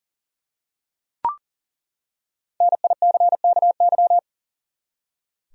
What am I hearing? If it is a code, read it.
Morse code, heard as a keyed tone, about 40 wpm, 700 Hz. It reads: NICKY
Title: USA Common Names - 40wpm